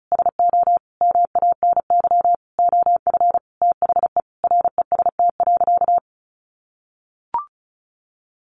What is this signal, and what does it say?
Morse code, heard as a keyed tone, about 34 wpm, 700 Hz. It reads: SO MANY OF THE REST.
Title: Sentences from Top 1500 Words 35wpm